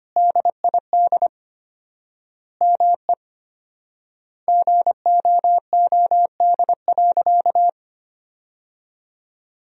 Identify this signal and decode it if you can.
Morse code, heard as a keyed tone, about 25 wpm, 700 Hz. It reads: DID ME GOOD.
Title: 4x Word-Spacing - Sentences from Top 200 Words - 25wpm